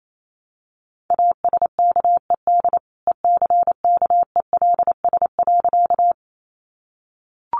Morse code, heard as a keyed tone, about 28 wpm, 700 Hz. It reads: ASKED ECKELS.
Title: A Sound Of Thunder 28wpm